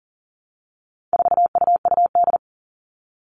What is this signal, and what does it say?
Morse code, heard as a keyed tone, about 40 wpm, 700 Hz. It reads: 4UUD
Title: All Cut Numbers - 4-Digit 40wpm